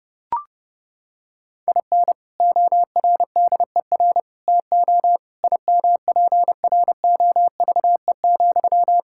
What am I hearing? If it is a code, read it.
Morse code, heard as a keyed tone, about 30 wpm, 700 Hz. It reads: IN ORDER TO IMPROVE,
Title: Sentences from Top 1000 Words - Review Part 5 - 30wpm